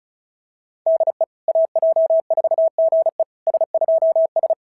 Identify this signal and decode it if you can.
Morse code, heard as a keyed tone, about 36 wpm, 690 Hz. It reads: DE AJ4GE S2S
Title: SOTA QSO Practice 35wpm